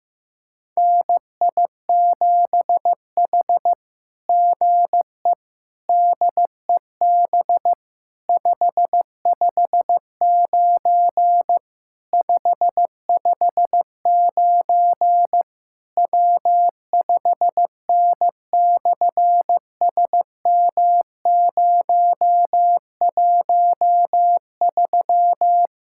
Morse code, heard as a keyed tone, about 15 wpm, 700 Hz. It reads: NI7H GE DEB 559 559 W5N/SM013
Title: SOTA QSO Practice 15wpm